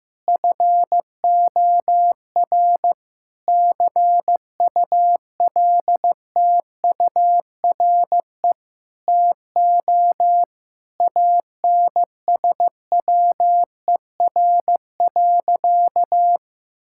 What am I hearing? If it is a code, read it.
Morse code, heard as a keyed tone, about 15 wpm, 700 Hz. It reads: FOR CULTURE TO ANSWER.
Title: Sentences from Top 1000 Words - Review Part 6 - 15wpm